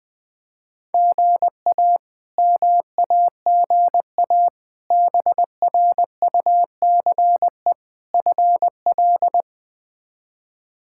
Morse code, heard as a keyed tone, about 21 wpm, 700 Hz. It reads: GA MAGA BRUCE FL
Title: K1USN SST QSO Practice - 20wpm